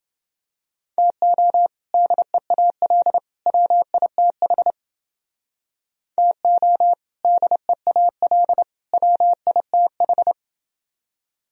Morse code, heard as a keyed tone, about 28 wpm, 700 Hz. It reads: TO DEAL WITH TO DEAL WITH